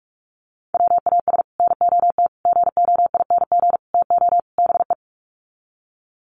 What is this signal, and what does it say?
Morse code, heard as a keyed tone, about 44 wpm, 700 Hz. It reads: WAS NOT GOING TO BE
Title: Sets of 5 Words Encore 45wpm